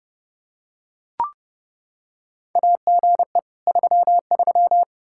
Morse code, heard as a keyed tone, about 32 wpm, 700 Hz. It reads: AGE 33
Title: K1USN SST QSO Practice 30wpm